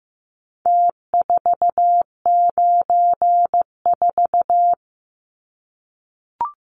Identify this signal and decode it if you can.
Morse code, heard as a keyed tone, about 15 wpm, 700 Hz. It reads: T494
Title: Numbers - Contest Style - 4-Digit 15wpm